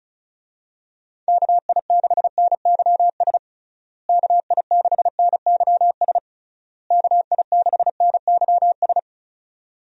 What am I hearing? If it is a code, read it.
Morse code, heard as a keyed tone, about 34 wpm, 700 Hz. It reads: KI6NYS KI6NYS KI6NYS
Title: USA Call Signs - 6 Characters - Repeated 3x - 35wpm